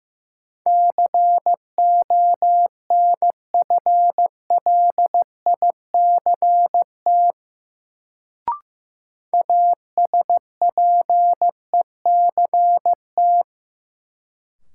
Morse code, heard as a keyed tone, about 16 wpm, 700 Hz. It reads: CONFLICT ASPECT
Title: Sentences from Top 1300 Words - New Word Drill Alt - 15wpm